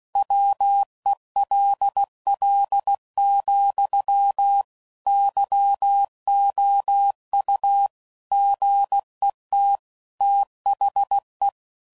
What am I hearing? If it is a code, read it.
Morse code, heard as a keyed tone, about 16 wpm, 800 Hz. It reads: WELL, YOU GET THE
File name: CW Code Groups @15WPM / Technical Subjects / AND Gate